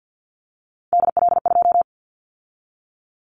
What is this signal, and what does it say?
Morse code, heard as a keyed tone, about 50 wpm, 700 Hz. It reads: DL2
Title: Callsign Prefixes for Top 50 DXCC - 50wpm